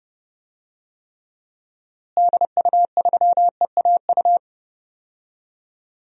Morse code, heard as a keyed tone, about 30 wpm, 700 Hz. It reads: DU3EAU
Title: DXCC-100 Call Signs - 6 Characters - Repeated 2x - 30wpm